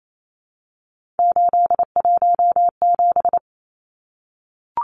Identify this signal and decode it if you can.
Morse code, heard as a keyed tone, about 28 wpm, 700 Hz. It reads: 817
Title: Numbers - 3-Digit 28wpm